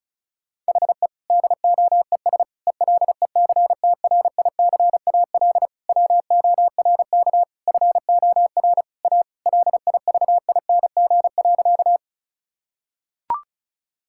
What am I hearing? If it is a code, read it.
Morse code, heard as a keyed tone, about 36 wpm, 700 Hz. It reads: HE DOES ELECTRICAL WORK FOR A LIVING.
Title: Sentences from Top 1000 Words - Review Part 6 - 35wpm